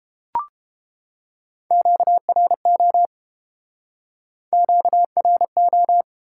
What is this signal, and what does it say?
Morse code, heard as a keyed tone, about 30 wpm, 700 Hz. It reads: QRO QRO